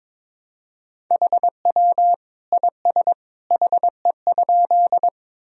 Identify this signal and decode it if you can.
Morse code, heard as a keyed tone, about 22 wpm, 700 Hz. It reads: HW IS HE?
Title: Sentences from Top 100 Words and Abbreviations 22wpm